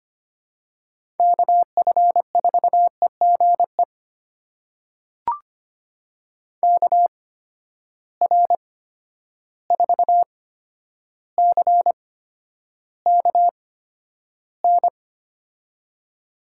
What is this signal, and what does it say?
Morse code, heard as a keyed tone, about 25 wpm, spaced as for 11 wpm, 700 Hz. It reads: KF4EGE KR4CKN
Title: USA Call Signs - 8x Character Spacing - 25wpm